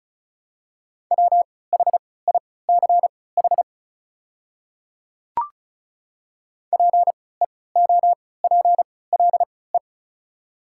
Morse code, heard as a keyed tone, about 34 wpm, spaced as for 20 wpm, 690 Hz. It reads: WHICH PEOPLE
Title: 3x Character Spacing - Top 200 Words - 35wpm